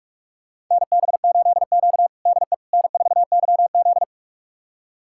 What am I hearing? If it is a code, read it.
Morse code, heard as a keyed tone, about 46 wpm, 700 Hz. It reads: NB8Q DE N4YZ
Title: USA Callsign DE Callsign - 4 Characters - 45wpm